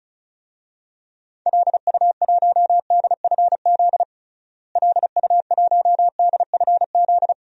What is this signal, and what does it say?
Morse code, heard as a keyed tone, about 36 wpm, 700 Hz. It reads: LU1DFZ LU1DFZ
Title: DXCC-100 Call Signs - 6 Characters - Repeated 2x - 35wpm